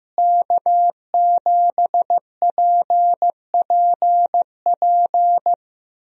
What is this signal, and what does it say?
Morse code, heard as a keyed tone, about 15 wpm, 700 Hz. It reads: K7PPP
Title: USA Call Signs - 5 Characters - Repeated 2x - 15wpm